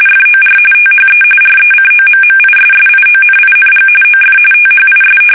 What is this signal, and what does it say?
RTTY, about 45.45 baud. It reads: FOR THE RECOMMISSIONING OF THIS
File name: fsk850 / battleshipnj